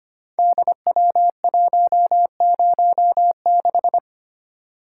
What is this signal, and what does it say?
Morse code, heard as a keyed tone, about 25 wpm, 700 Hz. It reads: DW106
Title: Code Groups - 5 Letter:Numbers - 25wpm